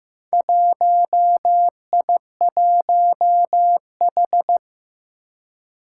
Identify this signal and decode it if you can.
Morse code, heard as a keyed tone, about 15 wpm, 700 Hz. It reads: 1I1H
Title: Four Challenging Letter-Number Combinations 15wpm